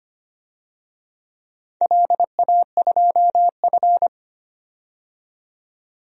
Morse code, heard as a keyed tone, about 25 wpm, 700 Hz. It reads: LA2F
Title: DXCC-100 Call Signs - 4 Characters - 25wpm